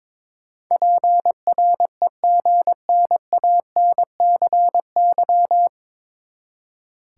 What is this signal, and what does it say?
Morse code, heard as a keyed tone, about 22 wpm, 700 Hz. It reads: PREGNANCY
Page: Morse Code Ninja practice 22wpm